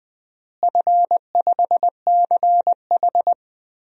Morse code, heard as a keyed tone, about 21 wpm, 700 Hz. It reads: F5CH
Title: DXCC-100 Call Signs - 4 Characters - Repeated 3x - 20wpm